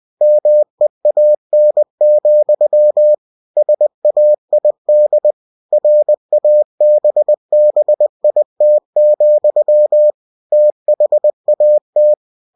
Morse code, heard as a keyed tone, about 20 wpm, 600 Hz. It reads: MEAN, SAID RABBIT, THAT